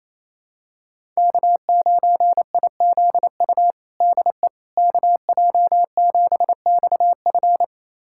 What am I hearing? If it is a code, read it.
Morse code, heard as a keyed tone, about 28 wpm, 700 Hz. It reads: K9IZU DE KJ7XF
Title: USA Callsign DE Callsign - 5 Characters - 28wpm